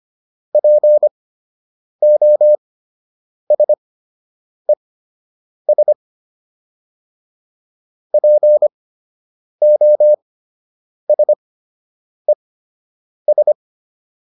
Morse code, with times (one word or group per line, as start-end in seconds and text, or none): 0.5-5.9 POSES
8.1-13.5 POSES